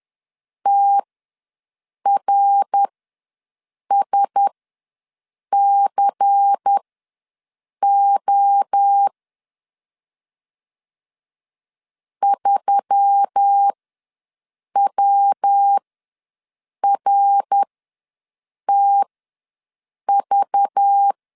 Morse code, heard as a keyed tone, about 10 wpm, spaced as for 6 wpm, 800 Hz. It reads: TRSCO 3WRTV
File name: LEZIONI / 5 LBVC / 5 LBVC ws 10wpm 3733